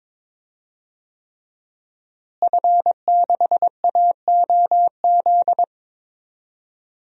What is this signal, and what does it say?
Morse code, heard as a keyed tone, about 22 wpm, 700 Hz. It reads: F6AOZ
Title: DXCC-100 Call Signs - 5 Characters - Repeated 3x - 22wpm